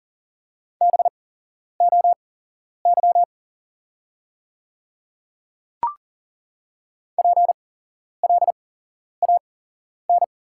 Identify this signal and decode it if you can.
Morse code, heard as a keyed tone, about 40 wpm, spaced as for 11 wpm, 690 Hz. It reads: BOY PLAN
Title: 8x Character Spacing - Top 500 Words - 40wpm